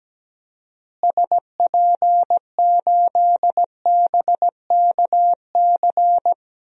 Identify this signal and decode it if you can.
Morse code, heard as a keyed tone, about 17 wpm, 700 Hz. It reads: SP8BKC
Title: DXCC-100 Call Signs - 6 Characters - Repeated 2x - 17wpm